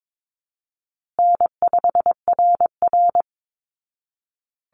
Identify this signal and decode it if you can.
Morse code, heard as a keyed tone, about 22 wpm, 700 Hz. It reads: N5RR
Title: Easy Call Signs Repeated 3x - 22wpm